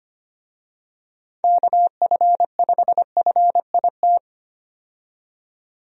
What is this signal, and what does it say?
Morse code, heard as a keyed tone, about 25 wpm, 700 Hz. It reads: KF5FIT
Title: USA Call Signs - 6 Characters - 25wpm